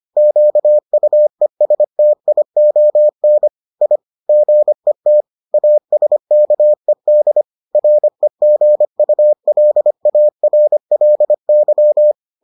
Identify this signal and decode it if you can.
Morse code, heard as a keyed tone, about 25 wpm, 600 Hz. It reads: QUESTION I GET ASKED REGULARLY